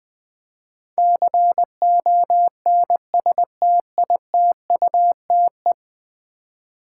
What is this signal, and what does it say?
Morse code, heard as a keyed tone, about 20 wpm, 700 Hz. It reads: CONSTITUTE
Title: 10 Letter Words - First 3 Letters Given - 20wpm